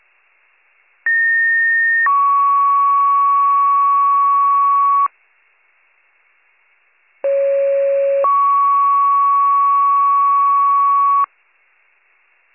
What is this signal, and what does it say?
Two-tone page, heard as a one-second tone, then a three-second tone, about 1850 then 1120 Hz, about 550 then 1120 Hz. It is heard from 1.1 to 11.3 s.